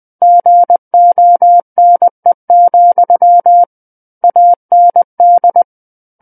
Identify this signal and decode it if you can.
Morse code, heard as a keyed tone, about 20 wpm, 700 Hz. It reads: GONE, AND
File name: 20 WPM